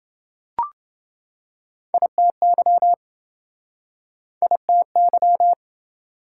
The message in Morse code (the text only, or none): ITY ITY